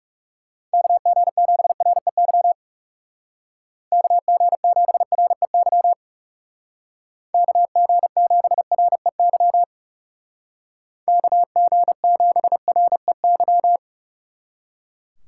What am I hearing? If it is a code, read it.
Morse code, heard as a keyed tone, about 37 wpm, 700 Hz. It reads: KG7REY KG7REY KG7REY KG7REY
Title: Speed Racing - USA 2x3 Call Signs 30wpm